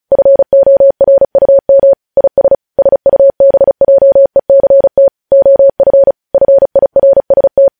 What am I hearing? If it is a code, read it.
Morse code, heard as a keyed tone, about 36 wpm, 550 Hz. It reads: FORUM IS SUBJECT OF FIRST
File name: podcast / News headlines-1x-35x35